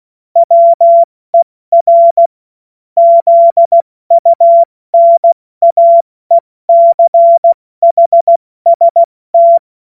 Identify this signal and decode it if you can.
Morse code, heard as a keyed tone, about 16 wpm, spaced as for 14 wpm, 680 Hz. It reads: WER ZUNAECHST
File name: Morsekurs / Witze / Witz 21.November 2023 70 F80